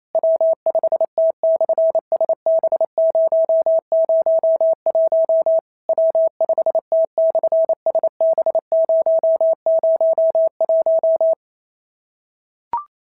Morse code, heard as a keyed tone, about 28 wpm, 650 Hz. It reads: W5T/SB001 W5T/SB001
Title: SOTA QSO Practice 28wpm